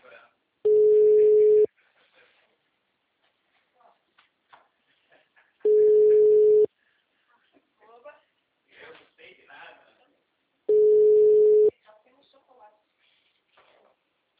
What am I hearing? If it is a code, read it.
Ringback tone (Europe, 425 Hz), with speech.